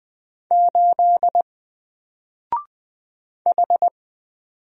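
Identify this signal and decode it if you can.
Morse code, heard as a keyed tone, about 20 wpm, 700 Hz. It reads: 8 H
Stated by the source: Single letter and number Morse drill at 20 wpm, from N4LCD.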